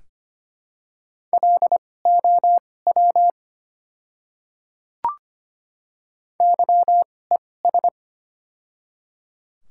Morse code, heard as a keyed tone, about 25 wpm, spaced as for 18 wpm, 700 Hz. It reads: LOW YES
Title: 2x Character Spacing - Top 500 Words - 25wpm